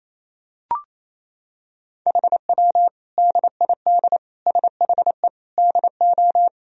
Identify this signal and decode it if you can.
Morse code, heard as a keyed tone, about 28 wpm, 700 Hz. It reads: HW DID SHE DO
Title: Sentences from Top 100 Words and Abbreviations 28wpm